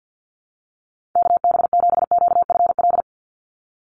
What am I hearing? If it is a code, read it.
Morse code, heard as a keyed tone, about 50 wpm, 700 Hz. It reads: KB7QFL
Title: USA Call Signs 50wpm